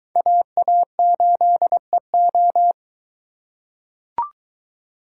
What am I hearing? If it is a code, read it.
Morse code, heard as a keyed tone, about 23 wpm, 700 Hz. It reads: AA8EO